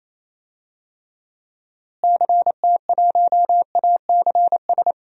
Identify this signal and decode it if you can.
Morse code, heard as a keyed tone, about 28 wpm, 700 Hz. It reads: CT1ACS
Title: DXCC-100 Call Signs - 6 Characters - Repeated 2x - 28wpm